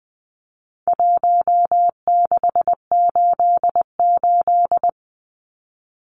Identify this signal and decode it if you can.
Morse code, heard as a keyed tone, about 20 wpm, 700 Hz. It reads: 1688
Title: Numbers - Contest Style - 4-Digit 20wpm